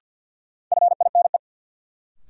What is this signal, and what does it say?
Morse code, heard as a keyed tone, about 50 wpm, 700 Hz. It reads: FINE
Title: Sentences from Top 500 Words - New Word Drill - 50wpm